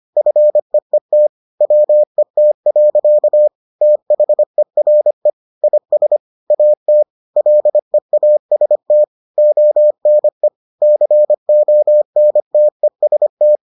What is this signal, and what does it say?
Morse code, heard as a keyed tone, about 25 wpm, 600 Hz. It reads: FEET WET. THERE IS AT LEAST ONE CONTEST